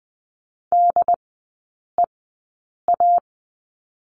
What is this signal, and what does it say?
Morse code, heard as a keyed tone, about 20 wpm, spaced as for 8 wpm, 700 Hz. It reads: DEA
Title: 152 Lesson 25e - Letter p - Three Characters 20wpm